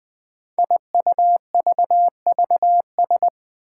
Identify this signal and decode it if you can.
Morse code, heard as a keyed tone, about 20 wpm, 700 Hz. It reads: IUVVS
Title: Five Challenging Letter-Number Combinations 20wpm